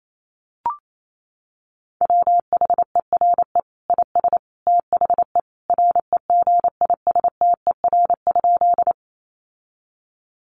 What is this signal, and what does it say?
Morse code, heard as a keyed tone, about 28 wpm, 700 Hz. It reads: WHERE IS THE REGISTER?
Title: Sentences from Top 1200 Words 28wpm